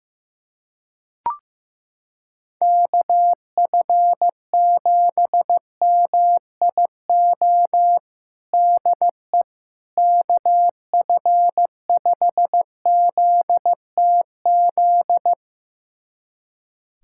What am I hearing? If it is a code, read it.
Morse code, heard as a keyed tone, about 16 wpm, 700 Hz. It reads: KF7MIO DE KF5ZTZ
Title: USA Callsign DE Callsign 15wpm